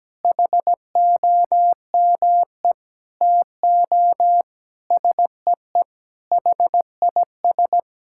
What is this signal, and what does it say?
Morse code, heard as a keyed tone, about 17 wpm, 700 Hz. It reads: HOME TO SEE HIS
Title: Sentences from Top 1000 Words - Review Part 6 - 17wpm